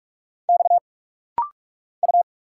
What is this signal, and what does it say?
Morse code, heard as a keyed tone, about 46 wpm, 700 Hz. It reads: X U
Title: Single Letters - Rapid-Fire 45wpm